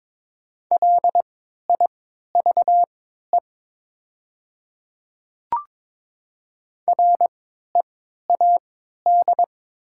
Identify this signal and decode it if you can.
Morse code, heard as a keyed tone, about 22 wpm, spaced as for 12 wpm, 700 Hz. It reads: LIVE READ
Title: 3x Character Spacing - Top 300 Words - 22wpm